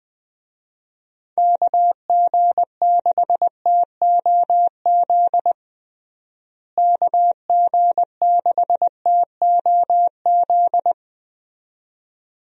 Morse code, heard as a keyed tone, about 20 wpm, 700 Hz. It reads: KG6TOZ KG6TOZ
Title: USA Call Signs - 6 Characters - Repeated 2x - 20wpm